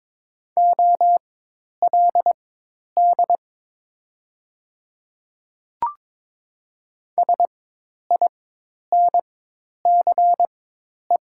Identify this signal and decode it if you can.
Morse code, heard as a keyed tone, about 22 wpm, spaced as for 10 wpm, 700 Hz. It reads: OLD SINCE
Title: 4x Character Spacing - Top 500 Words - 22wpm